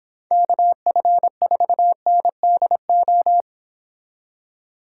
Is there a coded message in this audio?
Yes: Morse code from 0.3 to 3.4 s, heard as a keyed tone, about 29 wpm, 700 Hz. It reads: KF4NDO